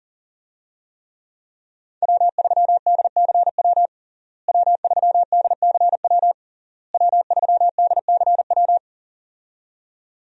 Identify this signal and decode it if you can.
Morse code, heard as a keyed tone, about 40 wpm, 700 Hz. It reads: W3DCW W3DCW W3DCW
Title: USA Call Signs - 5 Characters - Repeated 3x - 40wpm